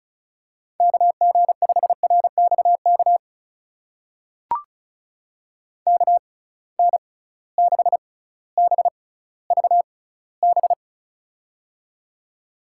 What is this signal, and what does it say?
Morse code, heard as a keyed tone, about 36 wpm, spaced as for 18 wpm, 700 Hz. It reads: KG5RXK KN6BVB
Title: USA Call Signs - 6x Character Spacing - 35wpm